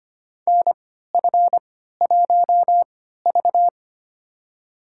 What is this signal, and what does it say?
Morse code, heard as a keyed tone, about 25 wpm, spaced as for 14 wpm, 700 Hz. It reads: NF1V
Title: USA Call Signs - 4 Characters - 3x Character Spacing - 25wpm